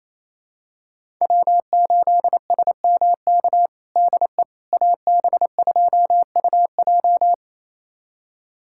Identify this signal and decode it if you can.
Morse code, heard as a keyed tone, about 28 wpm, 700 Hz. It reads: W8SMK DE AB2UJ
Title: USA Callsign DE Callsign - 5 Characters - 28wpm